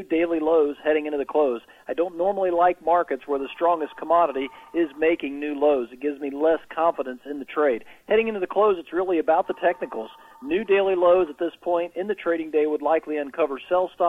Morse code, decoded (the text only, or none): none